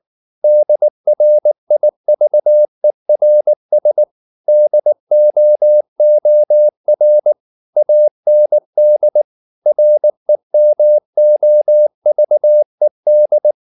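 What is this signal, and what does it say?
Morse code, heard as a keyed tone, about 19 wpm, 600 Hz. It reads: DRIVERS DOOR AND REMOVED
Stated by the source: Morse code at 19 wpm from CW med Gnister, reading Ken Follett's novel Never in English.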